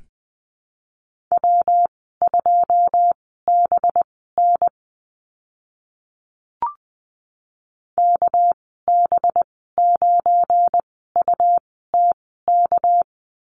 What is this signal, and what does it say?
Morse code, heard as a keyed tone, about 20 wpm, spaced as for 14 wpm, 700 Hz. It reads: W2BN KB9UTK
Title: 213 Lesson 34g - Number 8 - Call Signs 20wpm